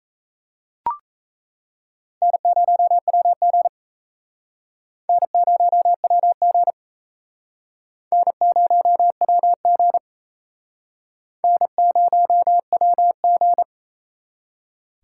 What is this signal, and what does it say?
Morse code, heard as a keyed tone, about 35 wpm, 700 Hz. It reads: N0WG N0WG N0WG N0WG